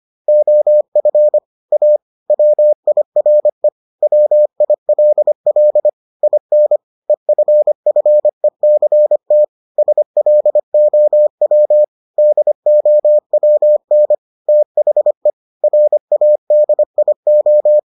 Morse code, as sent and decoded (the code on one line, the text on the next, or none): --- ..-. / .- / .-- .. .-. / .-- .. .-.. .-.. / .. -. / . ..-. ..-. -.-. - / ... .-.. --- .-- / -.. --- .-- -. / - .... / .-. .- -.. .. ---
OF A WIRE WILL IN EFFECT SLOW DOWN THE RADIO